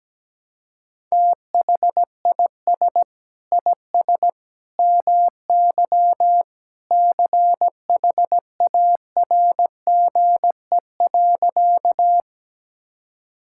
Morse code, heard as a keyed tone, about 17 wpm, 700 Hz. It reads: THIS IS MY CHARGE.
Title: Sentences from Top 1000 Words - Review Part 6 - 17wpm